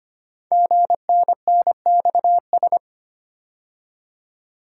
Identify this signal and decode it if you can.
Morse code, heard as a keyed tone, about 25 wpm, 700 Hz. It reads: GNNXS
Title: Code Groups - 5 Letters - 25wpm